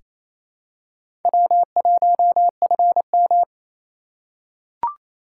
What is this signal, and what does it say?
Morse code, heard as a keyed tone, about 28 wpm, 700 Hz. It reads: W1FM
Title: USA Call Signs - 2x Character Spacing - 28wpm